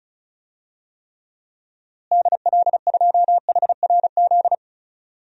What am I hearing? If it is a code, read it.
Morse code, heard as a keyed tone, about 36 wpm, 700 Hz. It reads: DL2HRZ
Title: DXCC-100 Call Signs - 6 Characters - Repeated 3x - 35wpm